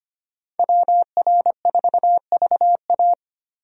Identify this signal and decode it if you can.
Morse code, heard as a keyed tone, about 25 wpm, 700 Hz. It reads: WR4VA
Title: USA Call Signs - 5 Characters - Repeated 2x - 25wpm